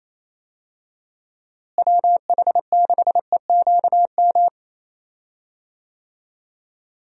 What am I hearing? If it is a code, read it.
Morse code, heard as a keyed tone, about 28 wpm, 700 Hz. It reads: WH6EQM
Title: USA Call Signs - 6 Characters - Repeated 2x - 28wpm